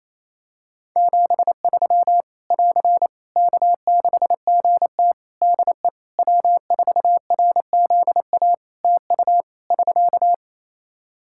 Morse code, heard as a keyed tone, about 28 wpm, 700 Hz. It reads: 73 <AR> K6GT DE W4RZA TU <SK>
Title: Practice QSO - Exchange 4 - Ending - 28wpm